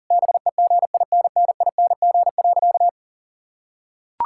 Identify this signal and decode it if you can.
Morse code, heard as a keyed tone, about 40 wpm, 700 Hz. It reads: BEGINNING.
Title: Sentences from Top 1000 Words - Review Part 5 - 40wpm